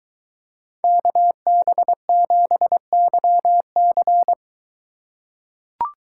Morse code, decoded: KB7YC

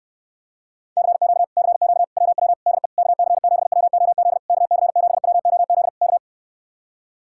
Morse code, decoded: CQ CQ WW DE KC7WOZ KC7WOZ K

